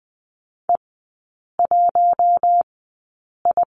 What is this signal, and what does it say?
Morse code, heard as a keyed tone, about 20 wpm, spaced as for 8 wpm, 700 Hz. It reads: E1I
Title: 041 Lesson 8e - Number 1 - Three Characters 20wpm